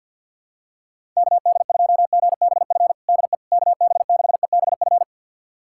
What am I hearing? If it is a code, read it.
Morse code, heard as a keyed tone, about 49 wpm, 700 Hz. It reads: KD2GBF DE KB6EBF
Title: USA Callsign DE Callsign 50wpm